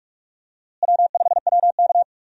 Morse code, heard as a keyed tone, about 43 wpm, 700 Hz. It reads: W5WK
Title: USA Call Signs - 4 Characters - Repeated 2x - 45wpm